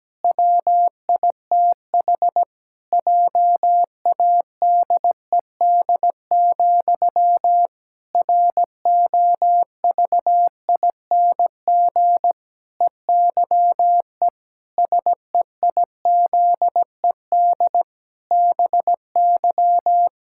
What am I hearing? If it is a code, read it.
Morse code, heard as a keyed tone, about 17 wpm, 700 Hz. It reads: WITH JADED, ROVING EYE SEIZED BY
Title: Pangram Challenge v2 - 17wpm